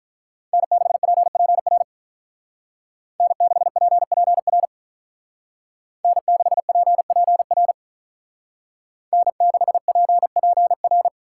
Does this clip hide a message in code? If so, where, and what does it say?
Yes: Morse code from 0.5 to 11.1 s, heard as a keyed tone, about 46 wpm, 700 Hz. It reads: N6PPR N6PPR N6PPR N6PPR